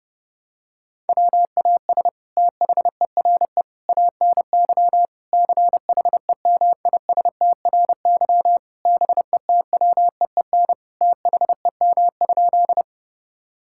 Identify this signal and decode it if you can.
Morse code, heard as a keyed tone, about 30 wpm, 700 Hz. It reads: WAS THERE ANY CHEMISTRY BETWEEN THEM?
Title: Sentences from Top 1000 Words - Review Part 6 - 30wpm